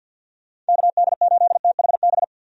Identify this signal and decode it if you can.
Morse code, heard as a keyed tone, about 47 wpm, 700 Hz. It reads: KD8THB
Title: USA Call Signs - 6 Characters - 50wpm